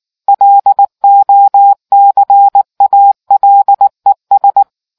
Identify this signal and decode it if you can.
Morse code, heard as a keyed tone, about 19 wpm, 800 Hz. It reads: LOCALES